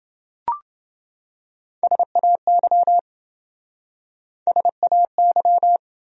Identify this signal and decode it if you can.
Morse code, heard as a keyed tone, about 27 wpm, 700 Hz. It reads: SAY SAY